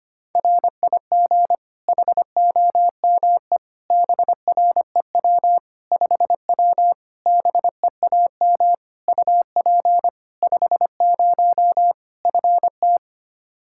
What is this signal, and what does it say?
Morse code, heard as a keyed tone, about 25 wpm, 700 Hz. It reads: RIG HOME BREW 5W BEAM UP 50 FT
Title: Practice QSO - Exchange 2 - Info - 25wpm